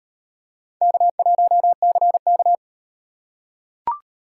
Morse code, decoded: K1CK

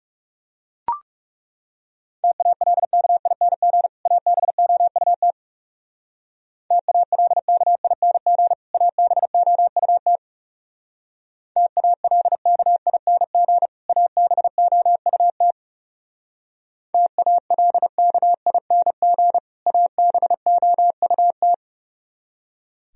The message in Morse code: TALKING ABOUT TALKING ABOUT TALKING ABOUT TALKING ABOUT

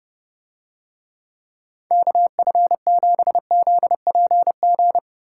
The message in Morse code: KF7ZPG

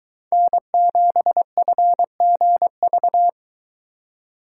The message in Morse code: N7FGV

